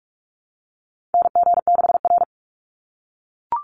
Morse code, heard as a keyed tone, about 44 wpm, 700 Hz. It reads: NG6R
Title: NA SOTA Chaser Callsigns 45wpm